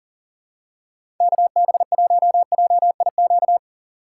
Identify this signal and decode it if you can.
Morse code, heard as a keyed tone, about 40 wpm, 700 Hz. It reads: KB1JIQ
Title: USA Call Signs - 4x Character Spacing - 40wpm